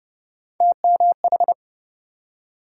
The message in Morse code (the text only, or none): TMH